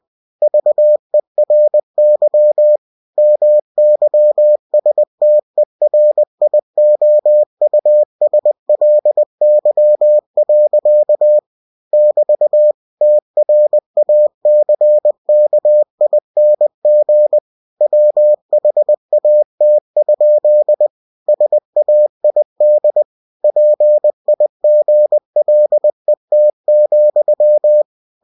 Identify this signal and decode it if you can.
Morse code, heard as a keyed tone, about 20 wpm, 600 Hz. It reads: VERY MYSTERIOUSLY. <BT> TRACKING WHAT? SAID PIGLET,
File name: winnie-20wpm